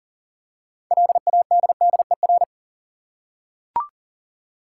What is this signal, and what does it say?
Morse code, heard as a keyed tone, about 40 wpm, 700 Hz. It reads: LADDER
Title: Six Letter Words - 40wpm